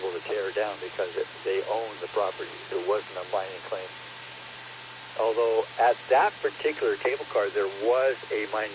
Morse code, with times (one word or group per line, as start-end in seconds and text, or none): none